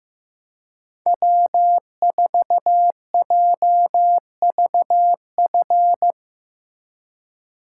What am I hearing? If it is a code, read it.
Morse code, heard as a keyed tone, about 17 wpm, 700 Hz. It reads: W4JVF
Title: Speed Racing - USA Call Signs 15wpm